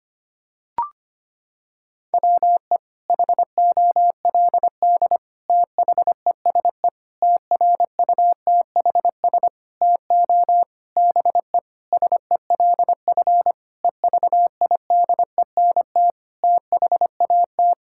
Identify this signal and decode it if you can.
Morse code, heard as a keyed tone, about 25 wpm, 700 Hz. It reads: WE HOLD THESE TRUTHS TO BE SELF EVIDENT THAT
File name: practice / 25wpm / Sentences with 9-Word Phrases - 25wpm